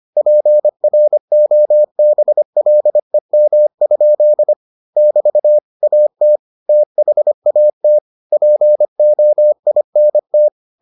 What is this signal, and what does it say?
Morse code, heard as a keyed tone, about 25 wpm, 600 Hz. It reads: PROBLEM? <BT> AT THAT POINT